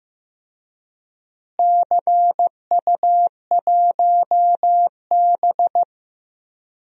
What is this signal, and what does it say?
Morse code, heard as a keyed tone, about 15 wpm, 700 Hz. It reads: CU1B
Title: DXCC-100 Call Signs - 4 Characters - Repeated 3x - 15wpm